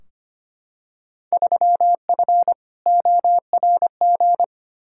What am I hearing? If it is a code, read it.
Morse code, heard as a keyed tone, about 25 wpm, 700 Hz. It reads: 3F ORG
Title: Field Day Exchanges - 25wpm